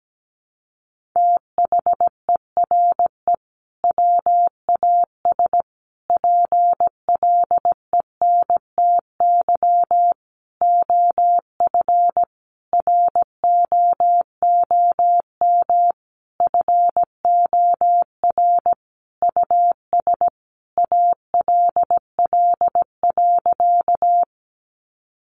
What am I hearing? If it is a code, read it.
Morse code, heard as a keyed tone, about 17 wpm, 700 Hz. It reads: THERE WAS PLENTY OF ROOM FOR US ALL.
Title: Sentences from Top 1300 Words 17wpm